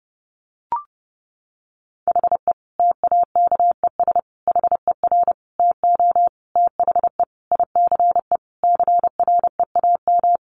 Morse code, heard as a keyed tone, about 30 wpm, 700 Hz. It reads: HE TAKES HER TO THE ICE CREAM
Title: Easy Phrases 30wpm